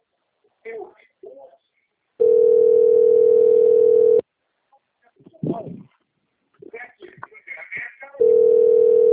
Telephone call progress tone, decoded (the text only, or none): ringback tone